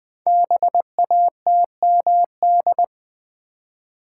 Morse code, heard as a keyed tone, about 20 wpm, 700 Hz. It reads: BATMD
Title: Code Groups - 5 Letters - 20wpm